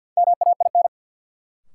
Morse code, heard as a keyed tone, about 50 wpm, 700 Hz. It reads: MAIN
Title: Sentences from Top 1000 Words - Review Part 2 - New Word Drill - 50wpm